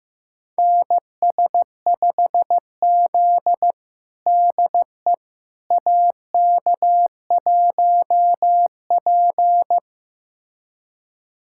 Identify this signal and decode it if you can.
Morse code, heard as a keyed tone, about 15 wpm, 700 Hz. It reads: NS5Z DE AK1P
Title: USA Callsign DE Callsign - 4 Characters - 15wpm